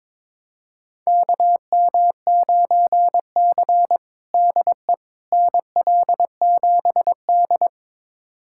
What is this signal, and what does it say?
Morse code, heard as a keyed tone, about 22 wpm, 700 Hz. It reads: KM9C DE NL7D
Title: USA Callsign DE Callsign - 4 Characters - 22wpm